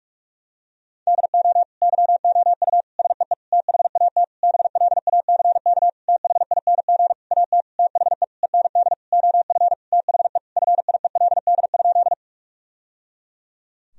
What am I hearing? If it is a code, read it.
Morse code, heard as a keyed tone, about 46 wpm, 700 Hz. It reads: DO YOU SEE THAT BLACK THING AT THE END OF THE FIELD?
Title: Sentences from Top 1000 Words - Review Part 2 - 45wpm